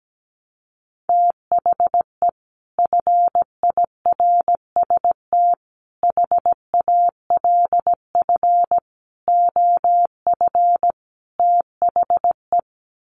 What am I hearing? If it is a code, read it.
Morse code, heard as a keyed tone, about 17 wpm, 700 Hz. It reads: THE FIRST HALF OF THE
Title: Sets of 5 Words 17wpm